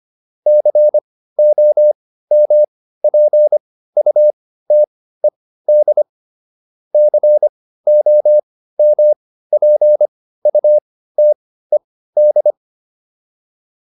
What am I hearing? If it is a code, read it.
Morse code, heard as a keyed tone, about 25 wpm, spaced as for 15 wpm, 600 Hz. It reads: COMPUTED COMPUTED